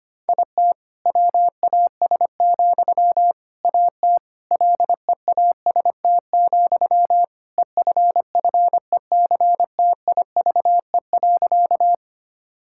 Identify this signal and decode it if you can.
Morse code, heard as a keyed tone, about 25 wpm, 700 Hz. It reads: IT WAS, AT LEAST, EFFECTIVE.
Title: Sentences from Top 1000 Words - Review Part 6 - 25wpm